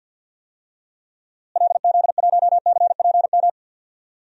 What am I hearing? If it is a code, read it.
Morse code, heard as a keyed tone, about 52 wpm, 700 Hz. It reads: LZ1CPM
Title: DXCC-100 Call Signs - 6 Characters - Repeated 3x - 50wpm